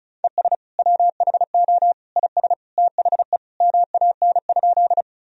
Morse code, heard as a keyed tone, about 36 wpm, 700 Hz. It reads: ES WHO IS THE MAN?
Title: Sentences from Top 200 Words and Abbreviations 35wpm